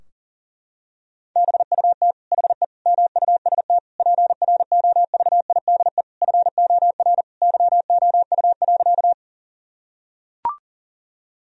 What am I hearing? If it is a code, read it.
Morse code, heard as a keyed tone, about 40 wpm, 700 Hz. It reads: BUT HE MUST PROVIDE FOR YOU.
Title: Sentences from Top 1000 Words - Review Part 3 - 40wpm